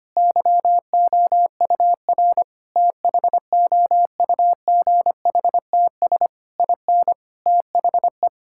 Morse code, heard as a keyed tone, about 25 wpm, 700 Hz. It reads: YOUR THOUGHTS IN THE
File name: practice / 25wpm / Sentences with 9-Word Phrases - 25wpm